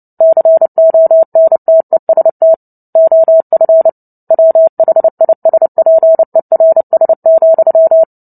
Morse code, heard as a keyed tone, about 30 wpm, 650 Hz. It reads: CONTEST OF WHISPERS,
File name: Book Two / 03-Chapter 3 - 29 WPM